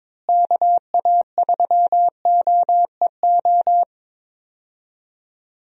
Morse code, heard as a keyed tone, about 22 wpm, 700 Hz. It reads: KA3OEO
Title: USA Call Signs - 6 Characters - Repeated 2x - 22wpm